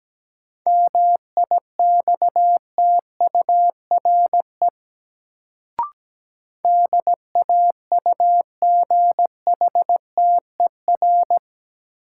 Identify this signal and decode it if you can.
Morse code, heard as a keyed tone, about 17 wpm, 700 Hz. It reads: MIXTURE DAUGHTER
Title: Sentences from Top 1500 Words - New Word Drill - 17wpm